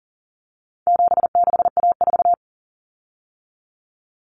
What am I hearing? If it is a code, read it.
Morse code, heard as a keyed tone, about 40 wpm, 700 Hz. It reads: 76A4
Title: Common Cut Numbers - 4-Digit 40wpm